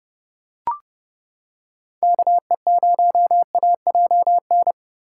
Morse code, heard as a keyed tone, about 29 wpm, 700 Hz. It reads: KE0AJN